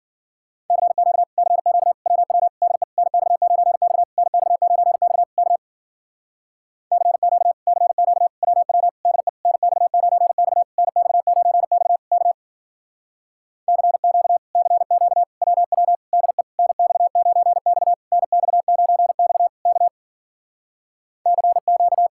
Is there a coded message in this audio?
Yes: Morse code from 0.7 to 22.1 s, heard as a keyed tone, about 52 wpm, 700 Hz. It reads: CQ CQ WW DE NX9X NX9X K CQ CQ WW DE NX9X NX9X K CQ CQ WW DE NX9X NX9X K CQ